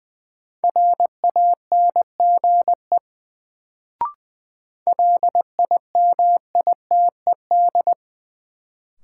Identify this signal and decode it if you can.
Morse code, heard as a keyed tone, about 20 wpm, 700 Hz. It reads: RANGE LIMITED
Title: Sentences from Top 900 Words - New Word Drill - 20wpm